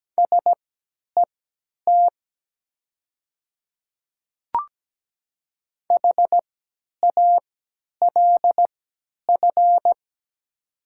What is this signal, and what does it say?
Morse code, heard as a keyed tone, about 17 wpm, spaced as for 10 wpm, 700 Hz. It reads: SET HALF
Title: 3x Character Spacing - Top 500 Words - 17wpm